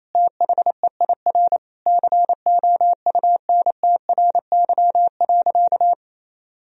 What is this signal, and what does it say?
Morse code, heard as a keyed tone, about 28 wpm, 700 Hz. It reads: THEIR COUNTRY.
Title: Sentences from Top 1000 Words - Review Part 1 - 28wpm